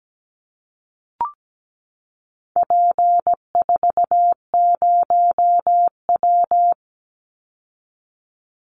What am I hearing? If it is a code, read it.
Morse code, heard as a keyed tone, about 17 wpm, 700 Hz. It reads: P40W
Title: Call Signs Repeated 3x - 17wpm